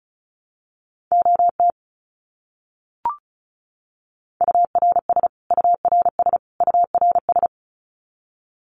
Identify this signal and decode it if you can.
Morse code, heard as a keyed tone, about 34 wpm, 700 Hz. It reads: OT URS URS URS
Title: Abbreviations Repeated 3x - 35wpm